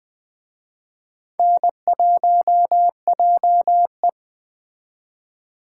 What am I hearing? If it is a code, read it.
Morse code, heard as a keyed tone, about 20 wpm, 700 Hz. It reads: N1JE